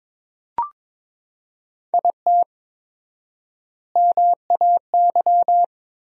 Morse code, heard as a keyed tone, about 21 wpm, 700 Hz. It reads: IT MAY
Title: 4x Word-Spacing - Sentences from Top 400 Words - 22wpm